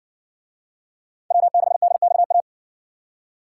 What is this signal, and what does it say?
Morse code, heard as a keyed tone, about 60 wpm, 700 Hz. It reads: W6DXA